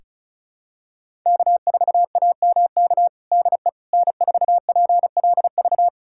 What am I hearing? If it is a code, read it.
Morse code, heard as a keyed tone, about 36 wpm, 700 Hz. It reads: K4AMK DE N4PLV
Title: USA Callsign DE Callsign - 5 Characters - 35wpm